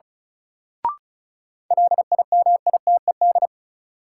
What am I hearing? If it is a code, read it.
Morse code, heard as a keyed tone, about 34 wpm, 700 Hz. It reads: LIMITED
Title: Sentences from Top 900 Words - New Word Drill - 35wpm